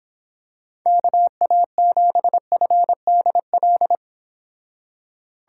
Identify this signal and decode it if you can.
Morse code, heard as a keyed tone, about 26 wpm, 700 Hz. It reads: KA7FDL